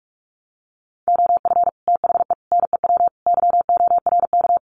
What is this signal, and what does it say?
Morse code, heard as a keyed tone, about 44 wpm, 700 Hz. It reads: OF THE NEW YORK
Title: Sets of 4 Words 45wpm